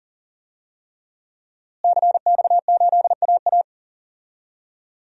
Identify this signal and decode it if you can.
Morse code, heard as a keyed tone, about 40 wpm, 700 Hz. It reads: CX8AA
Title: DXCC-100 Call Signs - 5 Characters - Repeated 3x - 40wpm